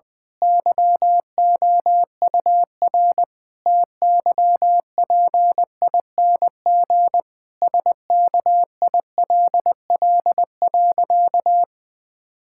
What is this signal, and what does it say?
Morse code, heard as a keyed tone, about 20 wpm, 700 Hz. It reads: YOUR TYPING SKILL.
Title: Pangram Challenge v2 - 20wpm